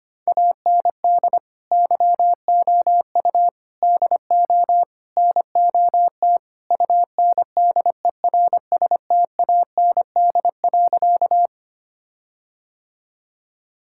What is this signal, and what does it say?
Morse code, heard as a keyed tone, about 25 wpm, 700 Hz. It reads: AND YOU DO NOT UNDERSTAND.
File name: practice / 25wpm / Sentences with 4-Word Phrases - 25wpm